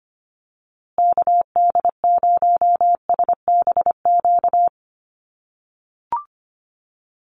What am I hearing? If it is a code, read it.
Morse code, heard as a keyed tone, about 25 wpm, 700 Hz. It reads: KD0SBQ
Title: USA Call Signs Repeated 3x 25wpm